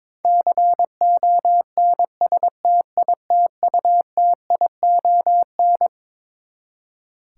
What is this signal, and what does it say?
Morse code, heard as a keyed tone, about 21 wpm, 700 Hz. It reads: CONSTITUTION
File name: Twelve Letter Words - 22wpm